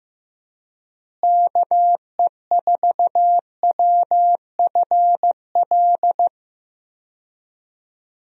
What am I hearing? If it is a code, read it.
Morse code, heard as a keyed tone, about 16 wpm, 700 Hz. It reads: KE4WFL
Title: USA Call Signs - 6 Characters - Repeated 3x - 15wpm